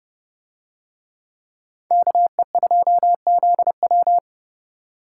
Morse code, heard as a keyed tone, about 29 wpm, 700 Hz. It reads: KE2ZW